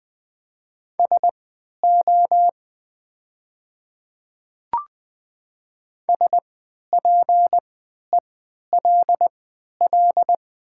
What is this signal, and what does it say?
Morse code, heard as a keyed tone, about 20 wpm, spaced as for 11 wpm, 700 Hz. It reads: SO SPELL
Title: 3x Character Spacing - Top 500 Words - 20wpm